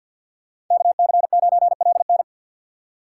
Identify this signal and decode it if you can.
Morse code, heard as a keyed tone, about 51 wpm, 700 Hz. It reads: KC9LN
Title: USA Call Signs - 5 Characters - 2x Character Spacing - 50wpm